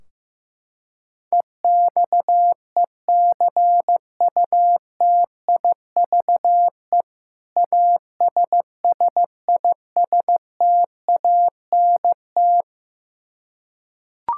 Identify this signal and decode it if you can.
Morse code, heard as a keyed tone, about 16 wpm, 700 Hz. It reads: EXECUTIVE ASSISTANT
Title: Practice QSO - Exchange 3 - More Info - 15wpm